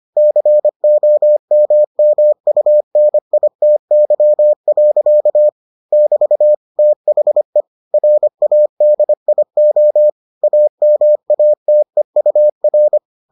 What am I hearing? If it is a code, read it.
Morse code, heard as a keyed tone, about 25 wpm, 600 Hz. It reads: COMMUNITY. <BT> THE RADIO AMATEUR